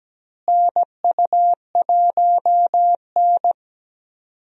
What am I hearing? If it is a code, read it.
Morse code, heard as a keyed tone, about 17 wpm, 700 Hz. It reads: NU1N
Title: USA Call Signs - 4 Characters - Repeated 3x - 17wpm